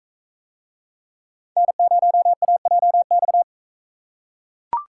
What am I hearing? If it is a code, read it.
Morse code, heard as a keyed tone, about 43 wpm, 700 Hz. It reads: N0AJX